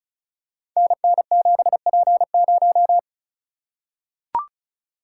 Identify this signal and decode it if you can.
Morse code, heard as a keyed tone, about 36 wpm, 700 Hz. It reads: NN7P0
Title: Code Groups - 5 Letter:Numbers - 35wpm